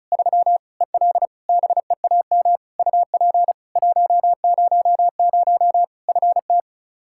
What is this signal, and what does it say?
Morse code, heard as a keyed tone, about 36 wpm, 700 Hz. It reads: 3 EL BEAM UP 100 FT
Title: Practice QSO - Exchange 2 - Info - 35wpm